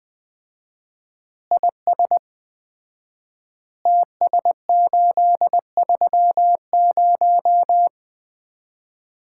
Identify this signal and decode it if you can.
Morse code, heard as a keyed tone, about 20 wpm, 700 Hz. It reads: IS TS830
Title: Common QSO Phrases - 4x Word-Spacing - 20wpm